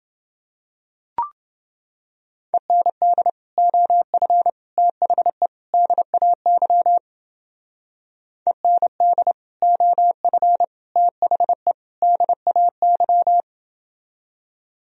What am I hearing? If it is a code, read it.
Morse code, heard as a keyed tone, about 28 wpm, 700 Hz. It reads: END OF THE DAY END OF THE DAY